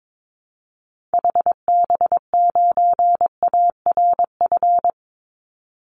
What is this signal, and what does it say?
Morse code, heard as a keyed tone, about 22 wpm, 700 Hz. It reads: HB9ARF
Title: Call Signs - 22wpm